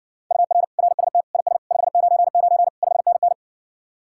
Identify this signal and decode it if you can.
Morse code, heard as a keyed tone, about 60 wpm, 700 Hz. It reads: UR RST IS 599 5NN